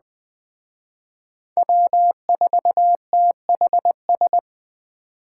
Morse code, heard as a keyed tone, about 21 wpm, 700 Hz. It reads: W4THS